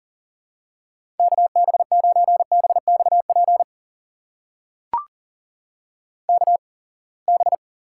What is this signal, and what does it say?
Morse code, heard as a keyed tone, about 40 wpm, spaced as for 28 wpm, 700 Hz. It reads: KB9BXP KB
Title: USA Call Signs - 6 Characters - 8x Character Spacing - 40wpm